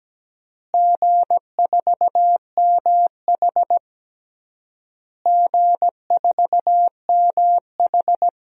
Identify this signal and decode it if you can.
Morse code, heard as a keyed tone, about 17 wpm, 700 Hz. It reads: G4MH G4MH
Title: DXCC-100 Call Signs - 4 Characters - Repeated 3x - 17wpm